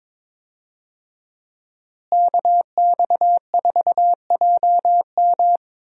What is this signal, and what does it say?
Morse code, heard as a keyed tone, about 22 wpm, 700 Hz. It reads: KX4JM
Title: USA Call Signs - 5 Characters - Repeated 3x - 22wpm